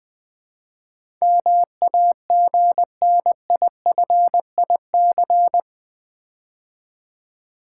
Morse code, heard as a keyed tone, about 20 wpm, 700 Hz. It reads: MAGNIFIC
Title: Build A Word - 4 Letters At A Time - 11-Letter Words - 20wpm